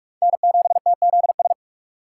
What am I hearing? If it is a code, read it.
Morse code, heard as a keyed tone, about 46 wpm, 700 Hz. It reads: N7TZS
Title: USA Call Signs - 5 Characters - 2x Character Spacing - 45wpm